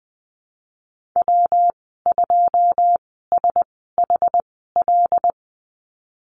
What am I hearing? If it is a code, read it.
Morse code, heard as a keyed tone, about 20 wpm, spaced as for 14 wpm, 700 Hz. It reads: W2SHL
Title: 186 Lesson 30g - Letter b - Call Signs 20wpm